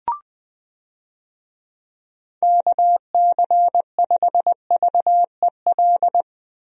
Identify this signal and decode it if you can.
Morse code, heard as a keyed tone, about 20 wpm, 700 Hz. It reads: KC5VEL
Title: USA Call Signs - 6 Characters - Repeated 3x - 20wpm